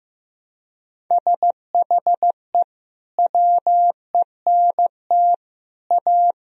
Morse code, heard as a keyed tone, about 15 wpm, 700 Hz. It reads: SHE WENT A
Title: Sentences from Top 1000 Words - Review Part 2 - 15wpm